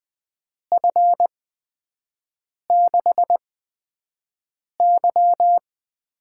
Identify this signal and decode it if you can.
Morse code, heard as a keyed tone, about 20 wpm, spaced as for 5 wpm, 700 Hz. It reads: F6Y
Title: USA Call Signs - 8x Character Spacing - 20wpm